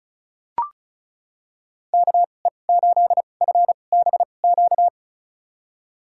Morse code, heard as a keyed tone, about 36 wpm, spaced as for 25 wpm, 700 Hz. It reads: KE8FBQ
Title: USA Call Signs - 6 Characters - 2x Character Spacing - 35wpm